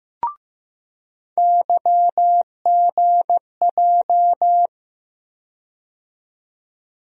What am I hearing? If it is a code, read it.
Morse code, heard as a keyed tone, about 15 wpm, 700 Hz. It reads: YGJ
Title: USA Call Sign Postfixes - 3 Letters 15wpm